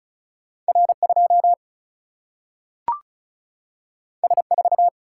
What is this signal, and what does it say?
Morse code, heard as a keyed tone, about 36 wpm, 700 Hz. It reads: R 2 S 4
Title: Single-Letter Single-Number 35wpm